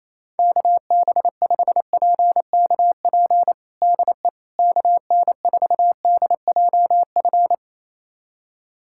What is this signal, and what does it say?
Morse code, heard as a keyed tone, about 28 wpm, 700 Hz. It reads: KB5PKP DE KN4DJF
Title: USA Callsign DE Callsign - 6 Characters - 28wpm